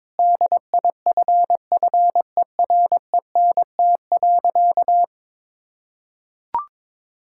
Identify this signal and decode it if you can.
Morse code, heard as a keyed tone, about 22 wpm, 700 Hz. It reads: DIFFERENT.
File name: practice / 22wpm / Sentences with 3-Word Phrases - 22wpm